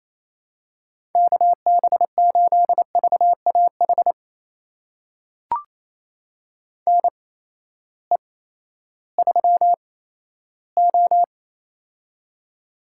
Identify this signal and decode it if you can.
Morse code, heard as a keyed tone, about 28 wpm, spaced as for 14 wpm, 700 Hz. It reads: KB8VAH NE3O
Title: USA Call Signs - 8x Character Spacing - 28wpm